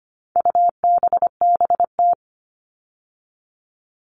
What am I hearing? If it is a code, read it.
Morse code, heard as a keyed tone, about 25 wpm, 700 Hz. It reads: UBBT
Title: All Cut Numbers - 4-Digit 25wpm